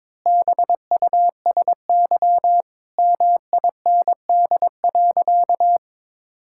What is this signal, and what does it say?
Morse code, heard as a keyed tone, about 22 wpm, 700 Hz. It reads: BUSY MIND.
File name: Difficult Sentences with Hints - 22wpm